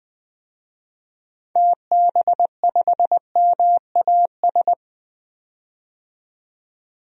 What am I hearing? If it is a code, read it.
Morse code, heard as a keyed tone, about 20 wpm, 700 Hz. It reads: TB5MAS
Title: DXCC-100 Call Signs - 6 Characters - 20wpm